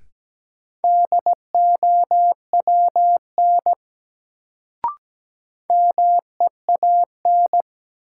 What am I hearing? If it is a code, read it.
Morse code, heard as a keyed tone, about 17 wpm, 700 Hz. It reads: DOWN MEAN
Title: Sets of 2 Words - New Word Drill - 17wpm